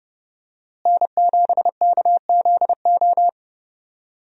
Morse code, heard as a keyed tone, about 30 wpm, 700 Hz. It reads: N7KZO
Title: USA Call Signs - 5 Characters - Repeated 3x - 30wpm